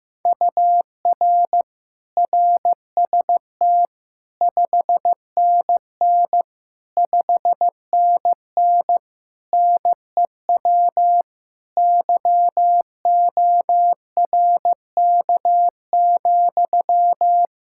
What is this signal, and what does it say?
Morse code, heard as a keyed tone, about 16 wpm, 700 Hz. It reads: UR RST 5NN 5NN NEW YORK,
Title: Practice QSO - Exchange 1 - Beginning - 15wpm